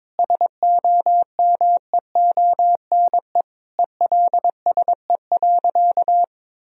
Morse code, heard as a keyed tone, about 22 wpm, 700 Hz. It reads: SOMEONE ELSE.
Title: Sentences from Top 1000 Words - Review Part 4 - 22wpm